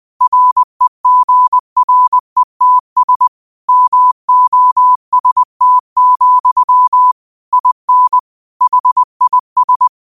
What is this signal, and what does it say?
Morse code, heard as a keyed tone, about 20 wpm, 1000 Hz. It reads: REGRETS MOST, IN HIS
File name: qotdfiles / 20wpm